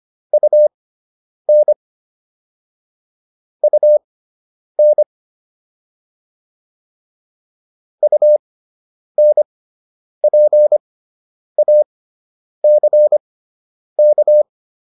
Morse code, heard as a keyed tone, about 25 wpm, spaced as for 9 wpm, 600 Hz. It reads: UN UN UNPACK